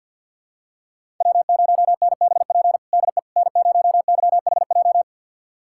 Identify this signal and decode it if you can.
Morse code, heard as a keyed tone, about 51 wpm, 700 Hz. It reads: W0NBP DE N0YHJ
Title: USA Callsign DE Callsign - 5 Characters - 50wpm